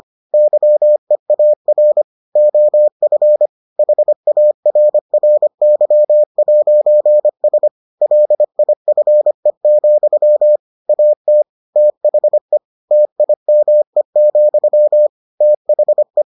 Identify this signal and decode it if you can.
Morse code, heard as a keyed tone, about 25 wpm, 600 Hz. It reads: YEAR OF HARRY'S LIFE, AT THE TIME, THE